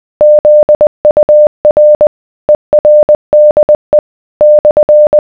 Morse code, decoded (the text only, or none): ZUR ERDE /